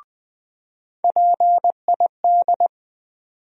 Morse code, heard as a keyed tone, about 20 wpm, 700 Hz. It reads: PID
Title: USA Call Sign Postfixes - 3 Letters 20wpm